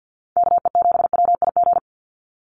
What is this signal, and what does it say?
Morse code, heard as a keyed tone, about 50 wpm, 700 Hz. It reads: KE7WIG
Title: USA Call Signs 50wpm